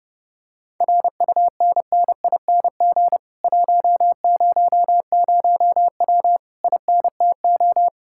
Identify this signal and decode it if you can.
Morse code, heard as a keyed tone, about 31 wpm, 700 Hz. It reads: RUNNING 100W INTO